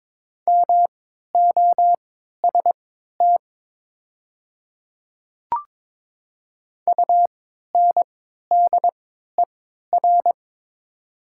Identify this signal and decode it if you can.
Morse code, heard as a keyed tone, about 22 wpm, spaced as for 13 wpm, 700 Hz. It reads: MOST UNDER
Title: 3x Character Spacing - Top 200 Words - 22wpm